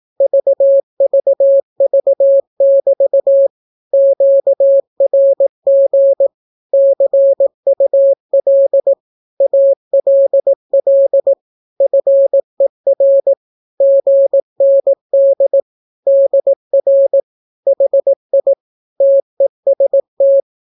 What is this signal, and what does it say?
Morse code, heard as a keyed tone, about 18 wpm, 550 Hz. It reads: VVV<BT> QRG CUL ALL FER GND DR HI TEST
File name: naukapodstaw / skroty / skroty